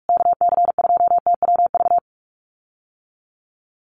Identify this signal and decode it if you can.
Morse code, heard as a keyed tone, about 44 wpm, 700 Hz. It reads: KC2TWV
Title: USA Call Signs Repeated 3x 45wpm